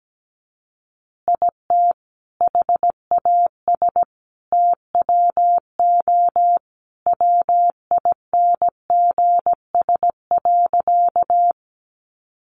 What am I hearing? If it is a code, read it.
Morse code, heard as a keyed tone, about 17 wpm, 700 Hz. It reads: IT HAS TWO WINGS.
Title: Additional Easy Phrases 17wpm